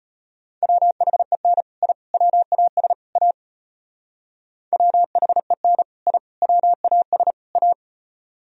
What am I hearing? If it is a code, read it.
Morse code, heard as a keyed tone, about 31 wpm, 700 Hz. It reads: WHEN I WAS A WHEN I WAS A
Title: Speed-Racing - Sets of 4 Words 25wpm